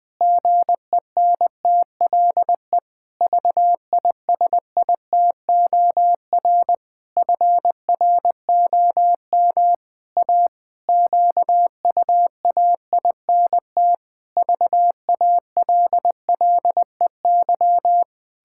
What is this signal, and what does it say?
Morse code, heard as a keyed tone, about 20 wpm, 700 Hz. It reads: GENTLE VISITOR FROM A QUAINT VALLEY
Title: Pangram Challenge v2 - 20wpm